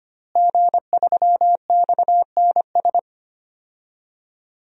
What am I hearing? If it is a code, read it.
Morse code, heard as a keyed tone, about 25 wpm, 700 Hz. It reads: G3XNS
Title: DXCC-100 Call Signs - 5 Characters - Repeated 3x - 25wpm